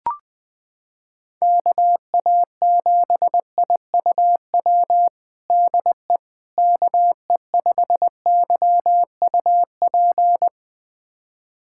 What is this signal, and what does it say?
Morse code, heard as a keyed tone, about 20 wpm, 700 Hz. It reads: KA7IUW DE KE5YUP
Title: USA Callsign DE Callsign - 6 Characters - 20wpm